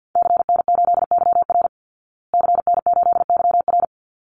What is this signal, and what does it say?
Morse code, heard as a keyed tone, about 50 wpm, 700 Hz. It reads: CN8YR CN8YR
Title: Call Signs Repeated 3x - 50wpm